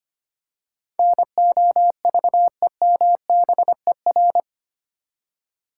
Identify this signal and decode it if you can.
Morse code, heard as a keyed tone, about 25 wpm, 700 Hz. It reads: NOVEMBER
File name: Eight Letter Words - 25wpm